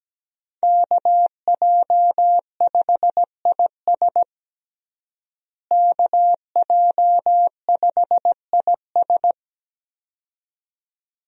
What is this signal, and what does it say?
Morse code, heard as a keyed tone, about 17 wpm, 700 Hz. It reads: KJ5IS KJ5IS
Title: USA Call Signs - 5 Characters - Repeated 2x - 17wpm